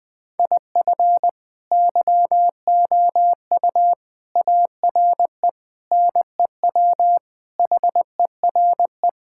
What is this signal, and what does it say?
Morse code, heard as a keyed tone, about 20 wpm, 700 Hz. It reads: IF YOU ARE NEW HERE
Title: Sentences with 8-Word Phrases - Spoken Beginning - 20wpm